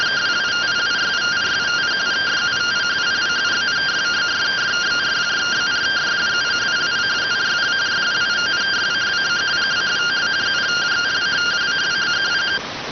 Two-tone teleprinter signal, about 170 Hz shift.